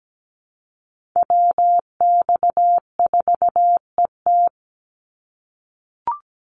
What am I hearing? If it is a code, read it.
Morse code, heard as a keyed tone, about 17 wpm, 700 Hz. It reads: WX4ET
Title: NA SOTA Chaser Callsigns 17wpm